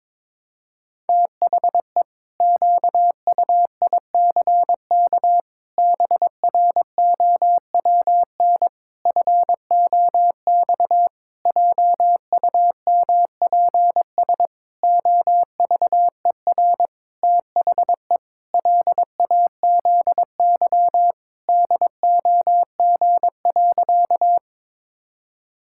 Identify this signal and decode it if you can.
Morse code, heard as a keyed tone, about 22 wpm, 700 Hz. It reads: THE QUICK BROWN FOX JUMPS OVER THE LAZY DOG.
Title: Pangram Challenge v2 - 22wpm